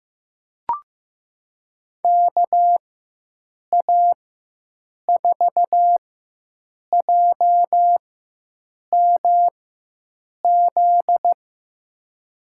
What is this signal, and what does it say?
Morse code, heard as a keyed tone, about 15 wpm, spaced as for 7 wpm, 700 Hz. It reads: KA4JMZ
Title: USA Call Signs - 6 Characters - 4x Character Spacing - 15wpm